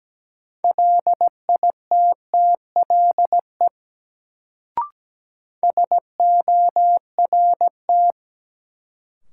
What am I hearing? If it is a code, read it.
Morse code, heard as a keyed tone, about 17 wpm, 700 Hz. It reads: LITTLE SORT
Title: Sets of 2 Words - New Word Drill - 17wpm